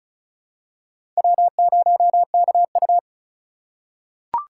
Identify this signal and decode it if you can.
Morse code, heard as a keyed tone, about 36 wpm, 700 Hz. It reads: W0KU
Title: USA Call Signs - 4 Characters - 4x Character Spacing - 35wpm